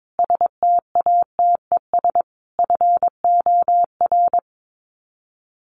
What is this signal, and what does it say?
Morse code, heard as a keyed tone, about 22 wpm, 700 Hz. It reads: STATES FOR
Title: Sets of 5 Words Encore2 22wpm